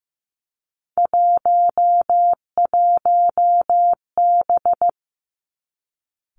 Morse code, heard as a keyed tone, about 15 wpm, 700 Hz. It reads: 11B
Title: Three Challenging Letter-Number Combinations 15wpm